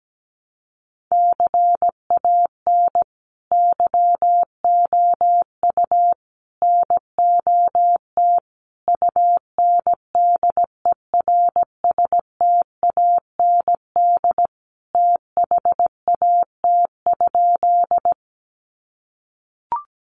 Morse code, heard as a keyed tone, about 17 wpm, 700 Hz. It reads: CAN YOU NOT UNDERSTAND THAT?
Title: Sentences from Top 600 Words 17wpm